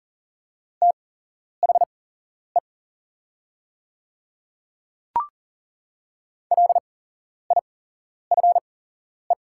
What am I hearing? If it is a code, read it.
Morse code, heard as a keyed tone, about 40 wpm, spaced as for 11 wpm, 700 Hz. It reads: THE LIFE
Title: 8x Character Spacing - Top 300 Words - 40wpm